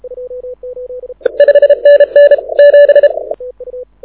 Morse code, heard as a keyed tone, about 31 wpm, 560 Hz. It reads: 5NN 7
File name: audio zawody / IARU 2015 CW SN7Q